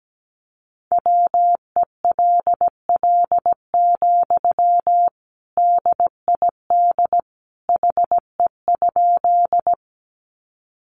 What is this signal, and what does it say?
Morse code, heard as a keyed tone, about 17 wpm, 700 Hz. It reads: WELL, DID HE?
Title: Sentences from Top 200 Words 17wpm